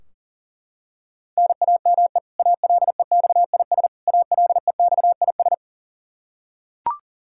Morse code, heard as a keyed tone, about 40 wpm, 700 Hz. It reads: NAME ALEXIS ALEXIS
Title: Practice QSO - Exchange 1 - Beginning - 40wpm